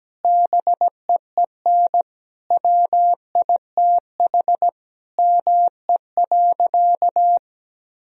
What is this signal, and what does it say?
Morse code, heard as a keyed tone, about 17 wpm, 700 Hz. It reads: BEEN WITH ME.
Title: Sentences from Top 1000 Words - Review Part 1 - 17wpm